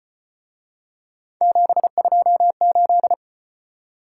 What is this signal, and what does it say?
Morse code, heard as a keyed tone, about 32 wpm, 700 Hz. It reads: 728